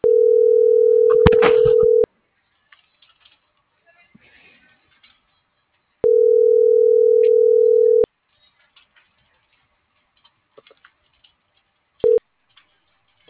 Ringback tone (North America), 440 + 480 Hz.